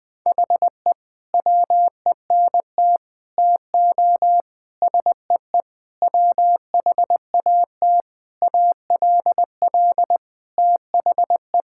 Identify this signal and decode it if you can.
Morse code, heard as a keyed tone, about 20 wpm, 700 Hz. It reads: HE WENT TO SEE WHAT ALL THE
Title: Sentences with 8-Word Phrases - Spoken Beginning - 20wpm